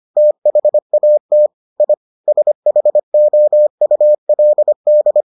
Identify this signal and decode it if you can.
Morse code, heard as a keyed tone, about 25 wpm, 600 Hz. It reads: THAT I SHOULD